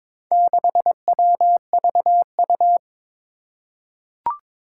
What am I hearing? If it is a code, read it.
Morse code, heard as a keyed tone, about 22 wpm, 700 Hz. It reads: 6WVU
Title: Four Challenging Letter-Number Combinations 22wpm